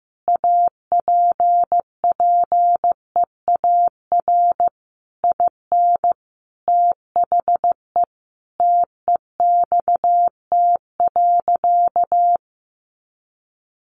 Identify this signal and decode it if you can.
Morse code, heard as a keyed tone, about 15 wpm, 700 Hz. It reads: APPEAR IN THE TEXT.
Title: Sentences from Top 1200 Words 15wpm